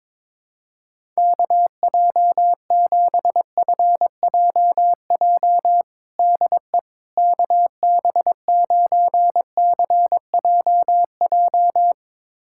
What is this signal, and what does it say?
Morse code, heard as a keyed tone, about 22 wpm, 700 Hz. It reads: KJ7FJJ DE KB9CJJ
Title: USA Callsign DE Callsign - 6 Characters - 22wpm